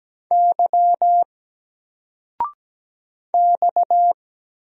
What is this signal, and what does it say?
Morse code, heard as a keyed tone, about 17 wpm, 700 Hz. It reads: Y X
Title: Single Letter-Number 17wpm